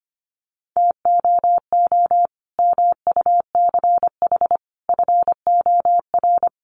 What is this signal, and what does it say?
Morse code, heard as a keyed tone, about 25 wpm, 700 Hz. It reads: TOO MUCH FOR
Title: Sentences from Top 200 Words 25wpm